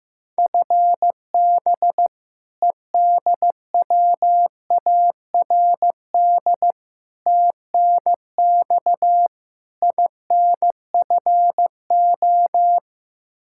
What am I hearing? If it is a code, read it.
Morse code, heard as a keyed tone, about 15 wpm, 700 Hz. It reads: FB EDWARD TNX INFO
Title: Practice QSO - Exchange 3 - More Info - 15wpm